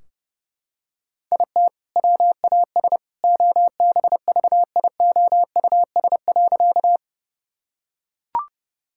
Morse code, heard as a keyed tone, about 30 wpm, 700 Hz. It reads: IT WAS OBVIOUS.
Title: Sentences from Top 1600 Words 30wpm